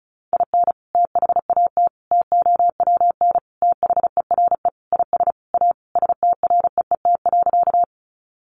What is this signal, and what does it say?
Morse code, heard as a keyed tone, about 34 wpm, 700 Hz. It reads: IN THAT TOWN THERE IS A STREET.
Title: Sentences from Top 500 Words 35wpm